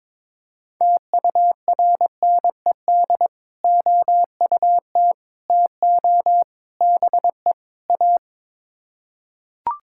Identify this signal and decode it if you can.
Morse code, heard as a keyed tone, about 22 wpm, 700 Hz. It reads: TURNED OUT TO BE A
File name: practice / 22wpm / Build A Phrase - 5-Word Phrases - 22wpm